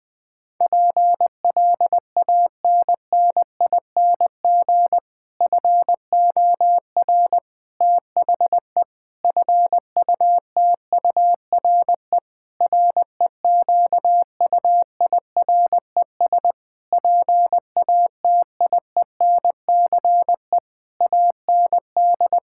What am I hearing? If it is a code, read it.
Morse code, heard as a keyed tone, about 20 wpm, 700 Hz. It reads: PLANNING FOR THE FUTURE REQUIRES PATIENCE AND
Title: Difficult Sentences with Hints - 20wpm